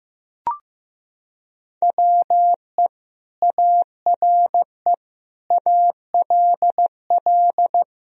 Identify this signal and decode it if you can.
Morse code, heard as a keyed tone, about 16 wpm, 700 Hz. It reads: WE ARE ALL
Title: Sentences from Top 1000 Words - Review Part 3 - 15wpm